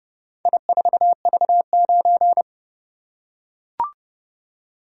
Morse code, heard as a keyed tone, about 31 wpm, 700 Hz. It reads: I4V9